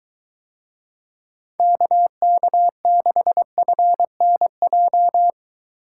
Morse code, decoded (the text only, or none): KK6FNJ